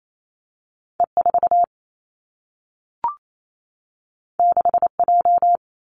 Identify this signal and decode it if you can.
Morse code, heard as a keyed tone, about 28 wpm, 700 Hz. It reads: E 4 6 J
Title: Two Challenging Letter-Number Combinations 28wpm